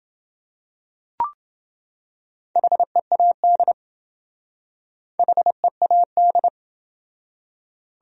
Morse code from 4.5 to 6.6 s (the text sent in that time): HEAD